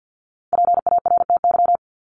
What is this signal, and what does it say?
Morse code, heard as a keyed tone, about 50 wpm, 700 Hz. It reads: PARTY
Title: ES:S Suffix - Top 100 Words with Base Words 50wpm